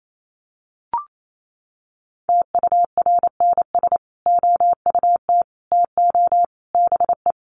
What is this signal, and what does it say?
Morse code, heard as a keyed tone, about 28 wpm, 700 Hz. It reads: TURNS OUT TO BE
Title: Sets of 5 Words Encore 28wpm